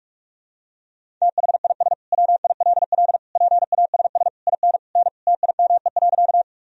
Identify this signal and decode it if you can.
Morse code, heard as a keyed tone, about 41 wpm, 700 Hz. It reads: THIS WILL PASS IN N TIME.
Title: Sentences from Top 400 Words and Abbreviations 45wpm